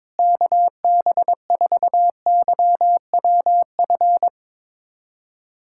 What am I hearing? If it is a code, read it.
Morse code, heard as a keyed tone, about 22 wpm, 700 Hz. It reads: KB4YWF
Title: USA Callsign DE Callsign - 6 Characters - 22wpm